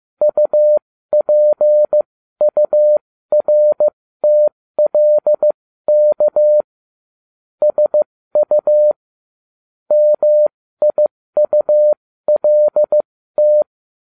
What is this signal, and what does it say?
Morse code, heard as a keyed tone, about 15 wpm, spaced as for 13 wpm, 600 Hz. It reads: UPURTLK SU MIULT
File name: cw / ubacw12 1